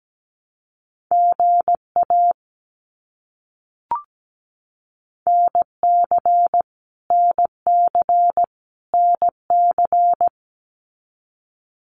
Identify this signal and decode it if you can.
Morse code, heard as a keyed tone, about 17 wpm, 700 Hz. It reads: GA NC NC NC
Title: US States Repeated 3x 17wpm